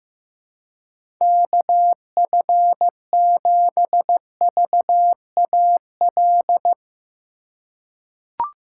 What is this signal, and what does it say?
Morse code, heard as a keyed tone, about 16 wpm, 700 Hz. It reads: KF7VAL
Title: USA Call Signs - 6 Characters - 4x Character Spacing - 15wpm